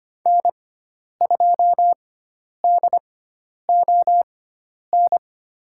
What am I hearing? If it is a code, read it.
Morse code, heard as a keyed tone, about 25 wpm, spaced as for 10 wpm, 700 Hz. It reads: N2DON
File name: Lesson 16g - Letter c - Call Signs 25wpm